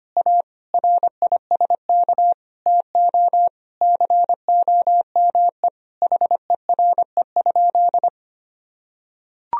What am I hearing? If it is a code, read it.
Morse code, heard as a keyed tone, about 25 wpm, 700 Hz. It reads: A RISK TO COME HERE?
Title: Sentences from Top 1000 Words - Review Part 4 - 25wpm